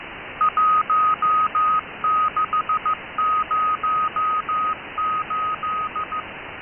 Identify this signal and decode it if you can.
Morse code, heard as a keyed tone, about 15 wpm, 1250 Hz, with noise. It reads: 1608
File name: Shindaisat cw